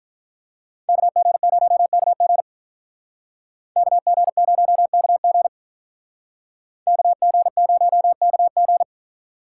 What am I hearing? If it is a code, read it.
Morse code, heard as a keyed tone, about 45 wpm, 700 Hz. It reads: KG0KG KG0KG KG0KG